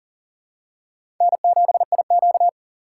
Morse code, heard as a keyed tone, about 40 wpm, 700 Hz. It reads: N7IQ
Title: USA Call Signs - 4 Characters - 4x Character Spacing - 40wpm